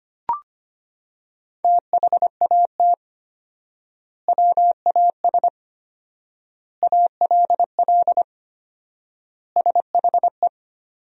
Morse code, heard as a keyed tone, about 25 wpm, 700 Hz. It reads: THAT WAS ALL SHE
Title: 4x Word-Spacing - Sentences from Top 100 Words - 25wpm